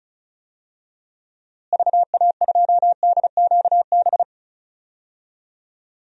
Morse code, heard as a keyed tone, about 36 wpm, 700 Hz. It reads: VA2DQB
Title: DXCC-100 Call Signs - 6 Characters - Repeated 3x - 35wpm